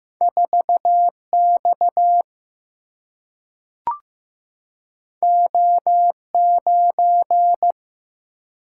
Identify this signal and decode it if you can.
Morse code, heard as a keyed tone, about 15 wpm, 700 Hz. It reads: 4 X O 9
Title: Single-Letter Single-Number 15wpm